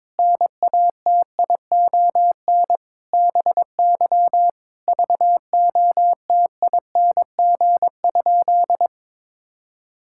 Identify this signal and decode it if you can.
Morse code, heard as a keyed tone, about 22 wpm, 700 Hz. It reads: NATION BY VOTING?
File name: Sentences with 9-Word Phrases - 22wpm